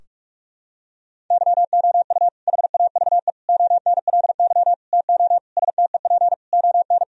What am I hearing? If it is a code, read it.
Morse code, heard as a keyed tone, about 46 wpm, 700 Hz. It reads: YOU HAVE ONLY TO STEP ON
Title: Sentences from Top 1000 Words - Review Part 4 - 45wpm